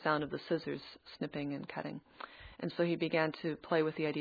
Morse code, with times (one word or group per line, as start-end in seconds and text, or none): none